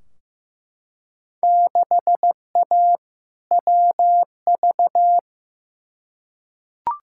Morse code, heard as a keyed tone, about 15 wpm, 700 Hz. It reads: 6A WV
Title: Field Day Exchanges - 15wpm